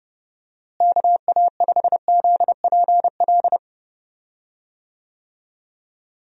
Morse code, heard as a keyed tone, about 30 wpm, 700 Hz. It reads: KA5ZPL
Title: USA Call Signs - 6 Characters - Repeated 2x - 30wpm